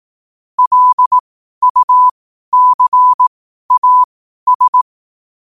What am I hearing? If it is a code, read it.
Morse code, heard as a keyed tone, about 18 wpm, spaced as for 12 wpm, 1000 Hz. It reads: LUCAS
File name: 12wpm